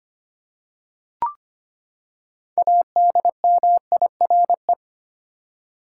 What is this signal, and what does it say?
Morse code, heard as a keyed tone, about 25 wpm, 700 Hz. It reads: ADMIRE
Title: Six Letter Words - 25wpm